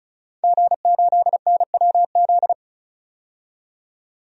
Morse code, heard as a keyed tone, about 36 wpm, 700 Hz. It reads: G8NWZ
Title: DXCC-100 Call Signs - 5 Characters - 35wpm